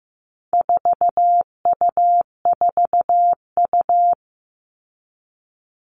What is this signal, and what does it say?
Morse code, heard as a keyed tone, about 15 wpm, 700 Hz. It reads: 4U4U
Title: All Cut Numbers - 4-Digit 15wpm